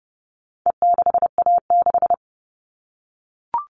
Morse code, heard as a keyed tone, about 30 wpm, 700 Hz. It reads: E6A6
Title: All Cut Numbers - 4-Digit 30wpm